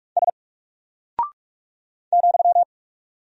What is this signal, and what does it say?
Morse code, heard as a keyed tone, about 44 wpm, 700 Hz. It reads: S ,
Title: Single Character (Letter:Number:Punctuation:Prosign) 45wpm